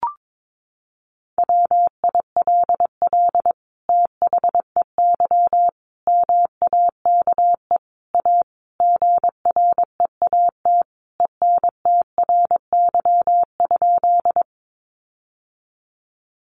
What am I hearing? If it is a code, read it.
Morse code, heard as a keyed tone, about 22 wpm, 700 Hz. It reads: WILL THEY MAKE A GREAT ENTRY?
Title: Sentences from Top 1500 Words 22wpm